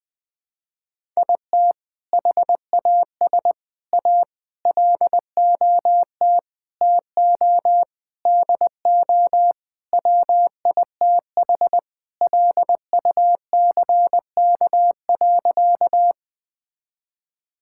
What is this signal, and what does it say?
Morse code, heard as a keyed tone, about 20 wpm, 700 Hz. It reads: IT HAS A LOT TO DO WITH LUCK.
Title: Sentences with 5-Word Phrases - Spoken Beginning - 20wpm